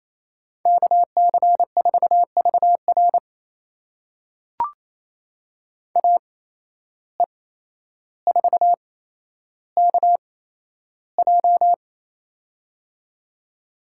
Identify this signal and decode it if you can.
Morse code, heard as a keyed tone, about 28 wpm, spaced as for 12 wpm, 700 Hz. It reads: KC4VR AE4KJ
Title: USA Call Signs - 5 Characters - 8x Character Spacing - 28wpm